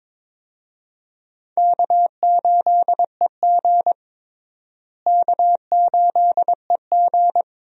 Morse code, heard as a keyed tone, about 22 wpm, 700 Hz. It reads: K8EG K8EG
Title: USA Call Signs - 4 Characters - Repeated 3x - 22wpm